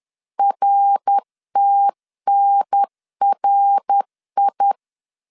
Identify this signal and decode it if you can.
Morse code, heard as a keyed tone, about 10 wpm, 800 Hz. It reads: RTNRI
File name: 10wpm